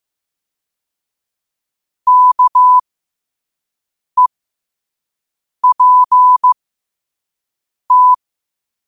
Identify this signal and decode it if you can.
Morse code, heard as a keyed tone, about 15 wpm, spaced as for 5 wpm, 1000 Hz. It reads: KEPT
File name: qotdfiles / 5wpm